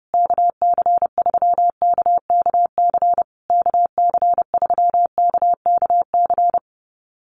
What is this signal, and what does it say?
Morse code, heard as a keyed tone, about 30 wpm, 700 Hz. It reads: KC3KKC KC3KKC
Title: USA Call Signs Repeated 3x 30wpm